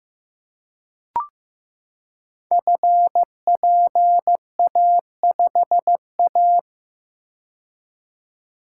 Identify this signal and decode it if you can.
Morse code, heard as a keyed tone, about 15 wpm, 700 Hz. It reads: FPA5A
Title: Code Groups - 5 Letter:Numbers - 15wpm